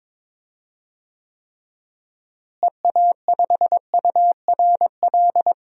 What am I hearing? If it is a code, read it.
Morse code, heard as a keyed tone, about 22 wpm, 700 Hz. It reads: EA5URL
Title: DXCC-100 Call Signs - 6 Characters - Repeated 2x - 22wpm